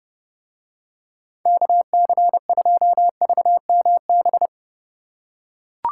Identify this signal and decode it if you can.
Morse code, heard as a keyed tone, about 30 wpm, 700 Hz. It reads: KC2VMB